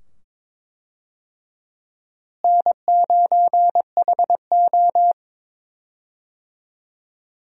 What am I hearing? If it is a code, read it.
Morse code, heard as a keyed tone, about 21 wpm, 700 Hz. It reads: N9HO
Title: USA Call Signs - 4 Characters - Repeated 2x - 22wpm